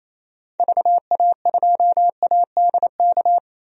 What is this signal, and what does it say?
Morse code, heard as a keyed tone, about 28 wpm, 700 Hz. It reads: VA2ADK
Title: DXCC-100 Call Signs - 6 Characters - 28wpm